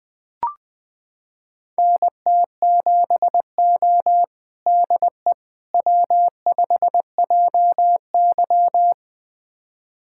Morse code, heard as a keyed tone, about 20 wpm, 700 Hz. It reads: NT7O DE W5JY
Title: USA Callsign DE Callsign - 4 Characters - 20wpm